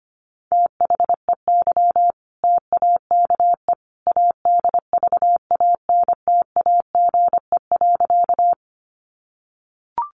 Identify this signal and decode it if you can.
Morse code, heard as a keyed tone, about 25 wpm, 700 Hz. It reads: THEY TAKE ADVANTAGE.
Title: Sentences from Top 900 Words 25wpm